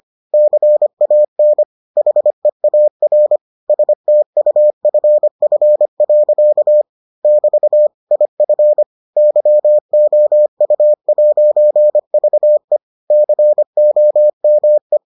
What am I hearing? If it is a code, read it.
Morse code, heard as a keyed tone, about 25 wpm, 600 Hz. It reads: CAN HEAR STUFF. <BT> IF YOU'VE COME